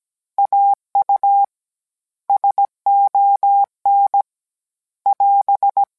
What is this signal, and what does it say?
Morse code, heard as a keyed tone, about 17 wpm, 800 Hz. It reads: AU SON <AS>